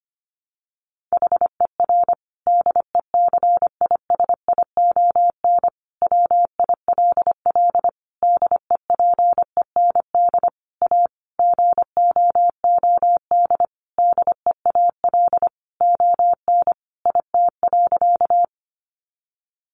Morse code, heard as a keyed tone, about 25 wpm, 700 Hz. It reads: HER DECISION WILL DEPEND A GOOD DEAL ON IT.
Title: Sentences from Top 1300 Words 25wpm